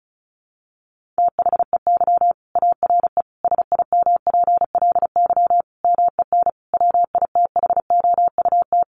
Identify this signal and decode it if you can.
Morse code, heard as a keyed tone, about 34 wpm, 700 Hz. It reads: THEY ARE SIMPLY MEN WITHOUT
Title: Sentences from Top 1500 Words 35wpm